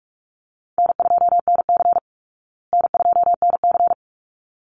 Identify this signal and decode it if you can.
Morse code, heard as a keyed tone, about 44 wpm, 700 Hz. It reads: N2NC N2NC
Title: Call Signs Repeated 3x - 45wpm